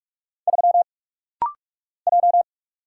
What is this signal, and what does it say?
Morse code, heard as a keyed tone, about 45 wpm, 700 Hz. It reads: 3 J
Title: Single Letter-Number Alt - Rapid-Fire 45wpm